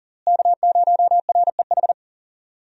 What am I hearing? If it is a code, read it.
Morse code, heard as a keyed tone, about 40 wpm, 700 Hz. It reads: K0REH
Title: USA Call Signs - 5 Characters - Repeated 2x - 40wpm